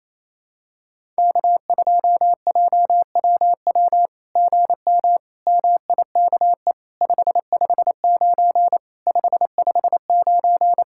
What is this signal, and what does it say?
Morse code, heard as a keyed tone, about 27 wpm, 700 Hz. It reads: K2JWW GM MIKE 559 559
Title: SOTA QSO Practice 28wpm